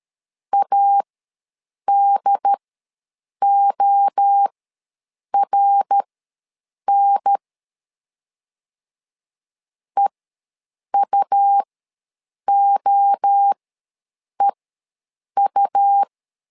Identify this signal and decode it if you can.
Morse code, heard as a keyed tone, about 13 wpm, spaced as for 7 wpm, 800 Hz. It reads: ADORN EUOEU
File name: LEZIONI / 3 MWRO / 3 MWRO ws 12wpm 1914